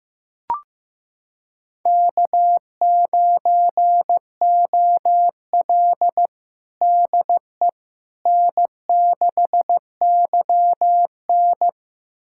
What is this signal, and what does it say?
Morse code, heard as a keyed tone, about 15 wpm, 700 Hz. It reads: K9OL DE N6YN
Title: USA Callsign DE Callsign - 4 Characters - 15wpm